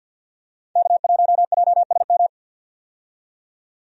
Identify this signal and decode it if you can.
Morse code, heard as a keyed tone, about 51 wpm, 700 Hz. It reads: K1JSM
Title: USA Call Signs - 5 Characters - 50wpm